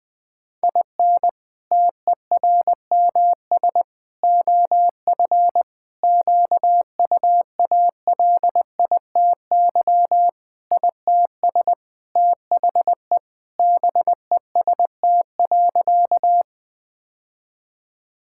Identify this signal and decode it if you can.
Morse code, heard as a keyed tone, about 20 wpm, 700 Hz. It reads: IN TERMS OF QUALITY ITS THE BEST.
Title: Sentences with 3-Word Phrases - Spoken Beginning - 20wpm